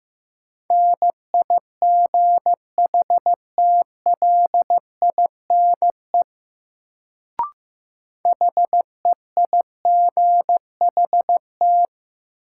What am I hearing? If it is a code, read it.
Morse code, heard as a keyed tone, about 15 wpm, 700 Hz. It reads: NIGHTLINE HEIGHT
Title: Sets of 5 Words - New Word Drill - 15wpm